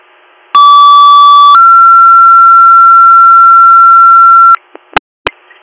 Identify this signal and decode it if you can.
Two-tone page, heard as a one-second tone, then a three-second tone, about 1150 then 1360 Hz, with noise. It is heard from 0.5 to 4.6 s.